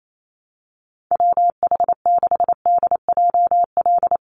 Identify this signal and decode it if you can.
Morse code, heard as a keyed tone, about 28 wpm, 700 Hz. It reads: WH6DJL
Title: USA Call Signs 28wpm